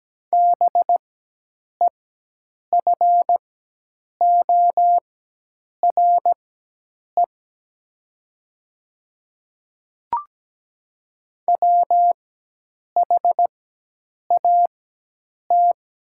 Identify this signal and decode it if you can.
Morse code, heard as a keyed tone, about 17 wpm, spaced as for 8 wpm, 700 Hz. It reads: BEFORE WHAT
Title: 4x Character Spacing - Top 300 Words - 17wpm